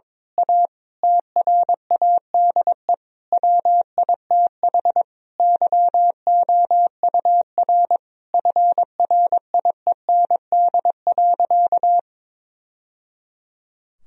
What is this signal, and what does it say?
Morse code, heard as a keyed tone, about 22 wpm, 700 Hz. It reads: A TRADE WITH YOUR FRIEND.
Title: Sentences from Top 1000 Words - Review Part 4 - 22wpm